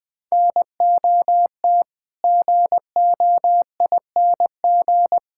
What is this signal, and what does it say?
Morse code, heard as a keyed tone, about 20 wpm, 700 Hz. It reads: NOT GOING
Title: Build A Phrase - 5-Word Phrases - 20wpm